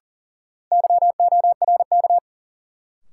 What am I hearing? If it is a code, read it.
Morse code, heard as a keyed tone, about 40 wpm, 700 Hz. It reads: YORK
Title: Sets of 2 Words - New Word Drill - 40wpm